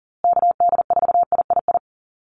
Morse code, heard as a keyed tone, about 40 wpm, 700 Hz. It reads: KD4III
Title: USA Call Signs Repeated 2x 40wpm